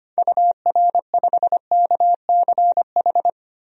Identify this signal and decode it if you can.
Morse code, heard as a keyed tone, about 25 wpm, 690 Hz. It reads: UR5KCH
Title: DXCC-100 Call Signs - 6 Characters - 25wpm